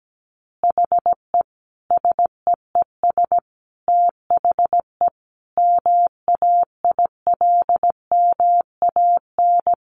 Morse code, heard as a keyed tone, about 17 wpm, 700 Hz. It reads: HE SEES THE MAILMAN
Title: Easy Phrases 17wpm